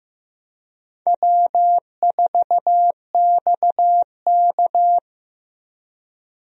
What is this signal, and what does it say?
Morse code, heard as a keyed tone, about 15 wpm, 700 Hz. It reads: W4XK
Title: USA Call Signs - 4 Characters - Repeated 3x - 15wpm